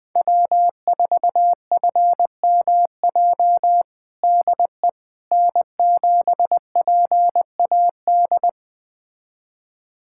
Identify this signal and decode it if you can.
Morse code, heard as a keyed tone, about 20 wpm, 700 Hz. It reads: W4FMJ DE N7PAD
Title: USA Callsign DE Callsign - 5 Characters - 20wpm